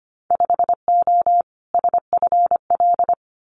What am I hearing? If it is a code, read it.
Morse code, heard as a keyed tone, about 25 wpm, 700 Hz. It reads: 5O SFL
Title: Winter Field Day Exchanges 25wpm